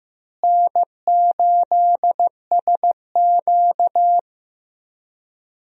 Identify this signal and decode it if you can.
Morse code, heard as a keyed tone, about 16 wpm, 700 Hz. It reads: N8SQ
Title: USA Call Signs - 4 Characters - Repeated 2x - 15wpm